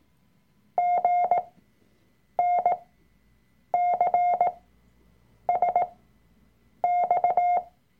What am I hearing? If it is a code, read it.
Morse code, heard as a keyed tone, about 18 wpm, spaced as for 7 wpm, 710 Hz. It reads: GNCSX
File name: Workshop Dateien / CW-Online-Kurs / Lektion 003